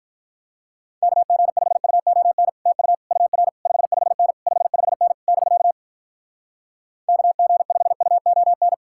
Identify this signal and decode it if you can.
Morse code, heard as a keyed tone, about 48 wpm, 700 Hz. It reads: KG5UON TU UR 55N 55N <BK> KG5UON